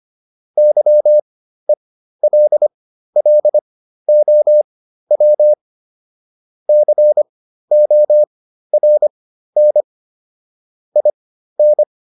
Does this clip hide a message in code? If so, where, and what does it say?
Yes: Morse code from 0.6 to 11.8 s, heard as a keyed tone, about 25 wpm, spaced as for 13 wpm, 600 Hz. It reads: YELLOW CORN IN